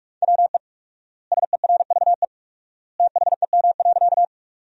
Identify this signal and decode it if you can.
Morse code, heard as a keyed tone, about 46 wpm, 700 Hz. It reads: WE SERVE THEM.
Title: 4x Word-Spacing - Sentences from Top 400 Words - 45wpm